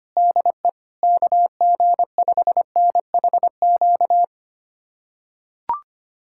Morse code, heard as a keyed tone, about 25 wpm, 700 Hz. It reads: DE KG5NHQ
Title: USA Callsign DE Callsign 25wpm